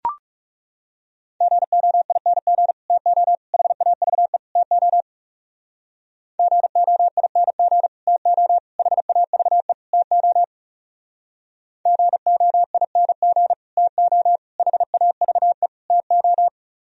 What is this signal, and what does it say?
Morse code, heard as a keyed tone, about 35 wpm, 700 Hz. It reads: GOING TO HAVE TO GOING TO HAVE TO GOING TO HAVE TO
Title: Speed-Racing - Sets of 4 Words 30wpm